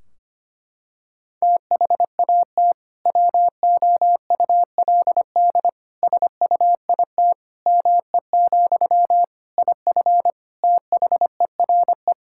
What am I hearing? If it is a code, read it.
Morse code, heard as a keyed tone, about 25 wpm, 700 Hz. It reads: THAT WOULD SUIT ME, IF THERE
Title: Sentences from Top 1600 Words 25wpm